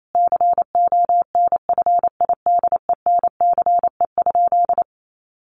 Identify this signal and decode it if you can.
Morse code, heard as a keyed tone, about 28 wpm, 700 Hz. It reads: CONFIDENCE?
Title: Sentences from Top 1500 Words 28wpm